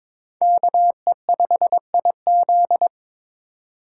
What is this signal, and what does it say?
Morse code, heard as a keyed tone, about 26 wpm, 700 Hz. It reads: KE5IZ
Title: Speed Racing - USA 2x2 Call Signs 22wpm